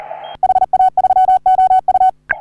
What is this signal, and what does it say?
Morse code, heard as a keyed tone, about 40 wpm, 750 Hz. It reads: HA3OU
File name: CW / HA3OU RHOBCW REC